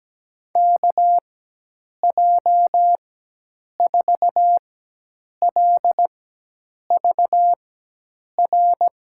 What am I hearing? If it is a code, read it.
Morse code, heard as a keyed tone, about 17 wpm, spaced as for 8 wpm, 700 Hz. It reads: KJ4LVR
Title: USA Call Signs - 6 Characters - 4x Character Spacing - 17wpm